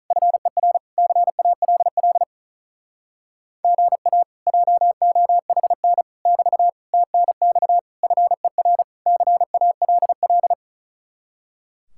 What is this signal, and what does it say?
Morse code, heard as a keyed tone, about 45 wpm, 700 Hz. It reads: FER CALL GA JOHN <BT> TNX FER CALL